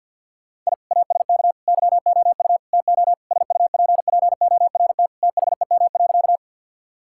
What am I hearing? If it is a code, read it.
Morse code, heard as a keyed tone, about 50 wpm, 700 Hz. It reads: I ASK YOU TO SUPPORT THEM.
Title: Sentences from Top 1000 Words - Review Part 5 - 50wpm